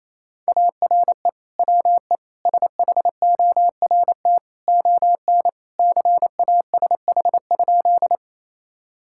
Morse code, heard as a keyed tone, about 28 wpm, 700 Hz. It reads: ARE WE SHORT ON CASH?
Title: Sentences from Top 1000 Words - Review Part 5 - 28wpm